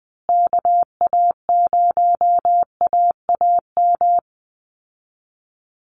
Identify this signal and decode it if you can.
Morse code, heard as a keyed tone, about 20 wpm, 700 Hz. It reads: KA0AAM
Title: Call Signs Repeated 2x - 20wpm